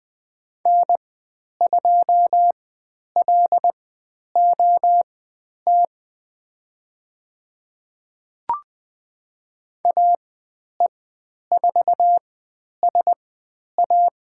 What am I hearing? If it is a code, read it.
Morse code, heard as a keyed tone, about 20 wpm, spaced as for 10 wpm, 700 Hz. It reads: N2LOT AE4SA
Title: 109 Lesson 18g - Letter m - Call Signs 20wpm